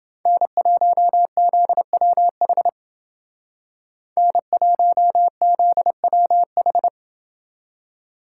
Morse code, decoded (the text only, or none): N1ZWH N1ZWH